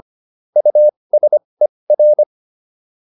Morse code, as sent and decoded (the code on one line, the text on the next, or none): ..- ... .-.
USER